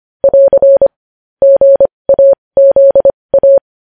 Morse code, heard as a keyed tone, about 25 wpm, spaced as for 20 wpm, 550 Hz. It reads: <AR> GAZA
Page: News Headlines in Morse Code at 20 WPM